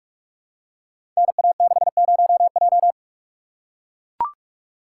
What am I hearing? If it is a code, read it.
Morse code, heard as a keyed tone, about 44 wpm, 700 Hz. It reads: NA60J
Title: Code Groups - 5 Letter:Numbers - 45wpm